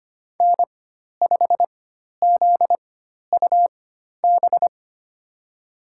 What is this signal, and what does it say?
Morse code, heard as a keyed tone, about 25 wpm, spaced as for 12 wpm, 700 Hz. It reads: N5ZUB
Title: USA Call Signs - 4x Character Spacing - 25wpm